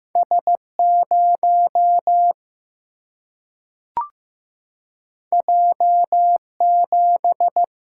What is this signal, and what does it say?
Morse code, heard as a keyed tone, about 15 wpm, 700 Hz. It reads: S 0 J 7
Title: Single-Letter Single-Number 15wpm